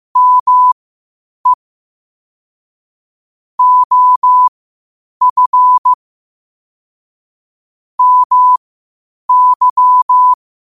Morse code, heard as a keyed tone, about 15 wpm, spaced as for 8 wpm, 1000 Hz. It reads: ME OF MY
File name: qotdfiles / 8wpm